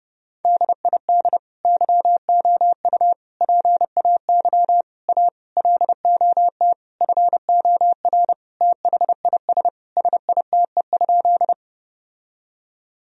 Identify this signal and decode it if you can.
Morse code, heard as a keyed tone, about 30 wpm, 700 Hz. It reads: DID YOU PAY A LOT FOR THIS SITE?
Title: Sentences from Top 1000 Words - Review Part 4 - 30wpm